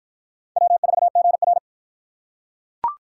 Morse code, heard as a keyed tone, about 44 wpm, 700 Hz. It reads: W4GR
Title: Speed Racing - USA 1x2 Call Signs 35wpm